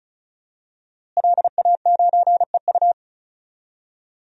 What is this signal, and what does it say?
Morse code, heard as a keyed tone, about 34 wpm, 700 Hz. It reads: LA9EU
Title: DXCC-100 Call Signs - 5 Characters - Repeated 2x - 35wpm